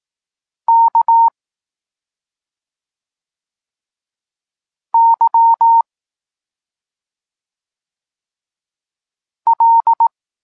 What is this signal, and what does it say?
Morse code, heard as a keyed tone, about 18 wpm, 920 Hz. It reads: K Y L